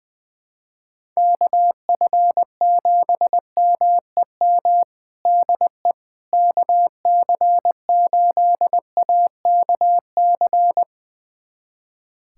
Morse code, heard as a keyed tone, about 21 wpm, 700 Hz. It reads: KF7MEM DE KC8AKC
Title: USA Callsign DE Callsign - 6 Characters - 20wpm